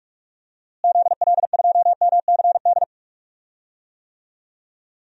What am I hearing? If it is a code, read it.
Morse code, heard as a keyed tone, about 42 wpm, 700 Hz. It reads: ZL2MCD
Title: DXCC-100 Call Signs - 6 Characters - Repeated 2x - 45wpm